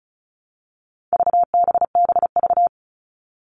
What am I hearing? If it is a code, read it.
Morse code, heard as a keyed tone, about 34 wpm, 700 Hz. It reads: VBBV
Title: All Cut Numbers - 4-Digit 35wpm